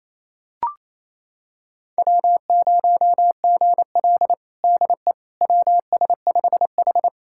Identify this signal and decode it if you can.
Morse code, heard as a keyed tone, about 28 wpm, 700 Hz. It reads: W0GL DE WS5H
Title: USA Callsign DE Callsign - 4 Characters - 28wpm